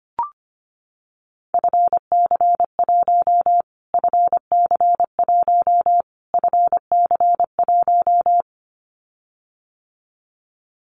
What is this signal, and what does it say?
Morse code, heard as a keyed tone, about 25 wpm, 700 Hz. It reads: FC1 FC1 FC1
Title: Callsign Prefixes for Top 25 DXCC - Repeated 3x - 25wpm